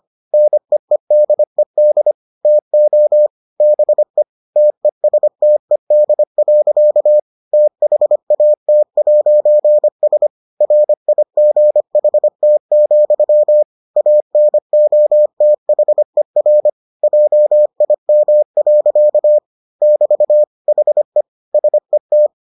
Morse code, heard as a keyed tone, about 25 wpm, 600 Hz. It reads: NEEDED TO BE TESTED. THAT'S RIGHT, ANOTHER JIM. <BT> HE SET